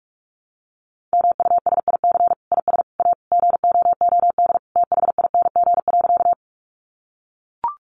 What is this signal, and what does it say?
Morse code, heard as a keyed tone, about 44 wpm, 700 Hz. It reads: MUSIC IS A GOOD THING.
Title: Sentences from Top 300 Words 45wpm